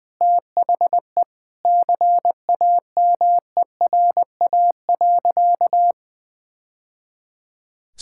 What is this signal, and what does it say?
Morse code, heard as a keyed tone, about 21 wpm, 700 Hz. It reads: THE CAMERA.
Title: Sentences with 3-Word Phrases - Spoken Beginning - 20wpm